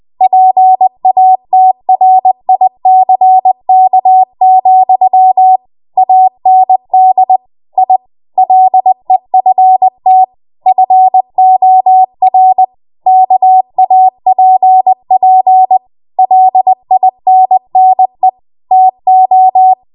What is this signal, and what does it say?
Morse code, heard as a keyed tone, about 20 wpm, 750 Hz. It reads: PATRICK, AND I LEFT FOR KAPP LINNE TO
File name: Morse / 20WPM